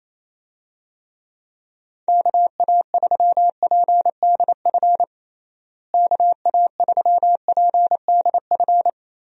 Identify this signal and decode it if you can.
Morse code, heard as a keyed tone, about 28 wpm, 700 Hz. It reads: KA3PDF KA3PDF
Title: USA Call Signs - 6 Characters - Repeated 3x - 28wpm